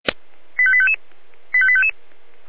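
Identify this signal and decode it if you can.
Five-tone selective call (ZVEI), heard as quick runs of five tones, tones of about 70 ms. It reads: 85855 85855
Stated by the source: Received on an SDR, ZVEI five-tone call 85369.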